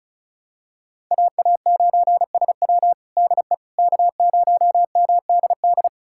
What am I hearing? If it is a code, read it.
Morse code, heard as a keyed tone, about 36 wpm, 700 Hz. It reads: AA9SW DE K0MDD
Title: USA Callsign DE Callsign 35wpm